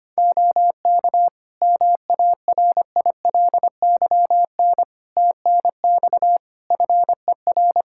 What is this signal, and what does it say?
Morse code, heard as a keyed tone, about 25 wpm, 700 Hz. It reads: OK MARILYN TNX FER
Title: Practice QSO - Exchange 4 - Ending - 25wpm